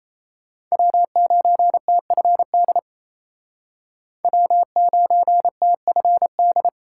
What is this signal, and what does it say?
Morse code, heard as a keyed tone, about 35 wpm, 700 Hz. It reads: W9TFD W9TFD
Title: Speed Racing - USA 1x3 Call Signs 28wpm